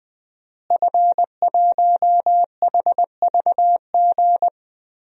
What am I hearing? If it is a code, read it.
Morse code, heard as a keyed tone, about 21 wpm, 700 Hz. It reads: F1HVG
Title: DXCC-100 Call Signs - 5 Characters - Repeated 2x - 20wpm